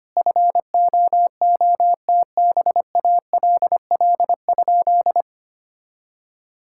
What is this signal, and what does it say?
Morse code, heard as a keyed tone, about 25 wpm, 700 Hz. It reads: FOOTBALL?
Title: Sentences from Top 1600 Words 25wpm